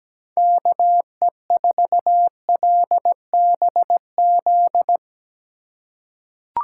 Morse code, heard as a keyed tone, about 17 wpm, 700 Hz. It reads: KE4LBZ
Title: USA Call Signs - 6 Characters - 2x Character Spacing - 17wpm